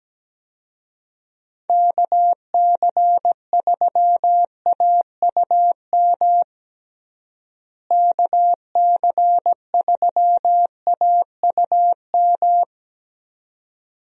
Morse code, heard as a keyed tone, about 17 wpm, 700 Hz. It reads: KC3AUM KC3AUM
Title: USA Call Signs - 6 Characters - Repeated 2x - 17wpm